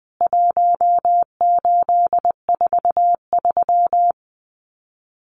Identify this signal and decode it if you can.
Morse code, heard as a keyed tone, about 20 wpm, 700 Hz. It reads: 1843
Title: CWT Exchange 20wpm